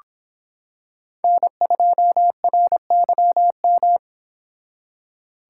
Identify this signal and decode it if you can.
Morse code, heard as a keyed tone, about 25 wpm, 700 Hz. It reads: N2RYM